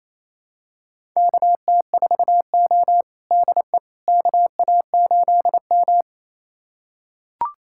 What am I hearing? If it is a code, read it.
Morse code, heard as a keyed tone, about 28 wpm, 690 Hz. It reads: KT4O DE KA8M
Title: USA Callsign DE Callsign - 4 Characters - 28wpm